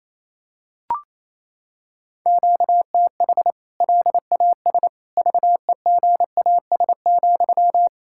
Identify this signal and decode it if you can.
Morse code, heard as a keyed tone, about 28 wpm, 700 Hz. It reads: QTH LAS VEGAS,
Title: K1USN SST QSO Practice 28wpm